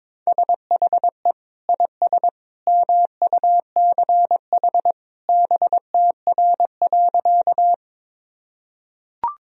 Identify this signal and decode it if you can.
Morse code, heard as a keyed tone, about 22 wpm, 700 Hz. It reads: SHE IS MUCH BTR.
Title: Sentences from Top 400 Words and Abbreviations 22wpm